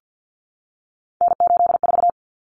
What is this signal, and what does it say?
Morse code, heard as a keyed tone, about 48 wpm, 700 Hz. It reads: N84
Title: Common Cut Numbers - 3-Digit 50wpm